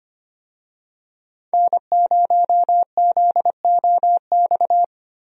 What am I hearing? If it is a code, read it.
Morse code, heard as a keyed tone, about 25 wpm, 700 Hz. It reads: N0ZOX
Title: USA Call Signs - 5 Characters - Repeated 2x - 25wpm